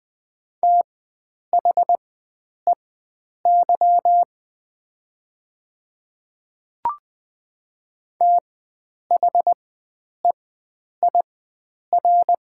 Morse code, heard as a keyed tone, about 20 wpm, spaced as for 9 wpm, 700 Hz. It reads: THEY THEIR
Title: 4x Character Spacing - Top 100 Words - 20wpm